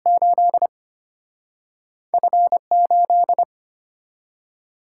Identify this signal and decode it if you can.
Morse code, heard as a keyed tone, about 34 wpm, 700 Hz. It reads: 8 F 8